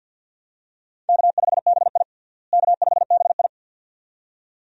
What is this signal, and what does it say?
Morse code, heard as a keyed tone, about 50 wpm, 700 Hz. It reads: K5BI K5BI
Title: USA Call Signs - 4 Characters - Repeated 2x - 50wpm